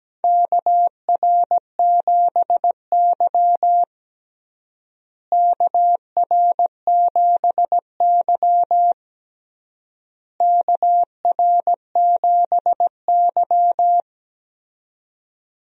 Morse code, heard as a keyed tone, about 17 wpm, 700 Hz. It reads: KR7Y KR7Y KR7Y
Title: USA Call Signs - 4 Characters - Repeated 3x - 17wpm